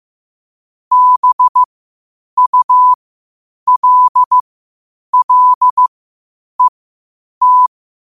Morse code, heard as a keyed tone, about 15 wpm, spaced as for 8 wpm, 1000 Hz. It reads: BULLET
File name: qotdfiles / 8wpm